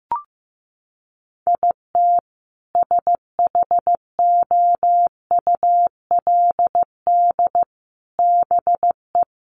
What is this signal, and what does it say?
Morse code, heard as a keyed tone, about 15 wpm, 700 Hz. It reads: IT SHOULD BE
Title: Sets of 5 Words 15wpm